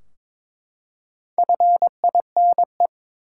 Morse code, heard as a keyed tone, about 22 wpm, 700 Hz. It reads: FINE
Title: Four Letter Words - 22wpm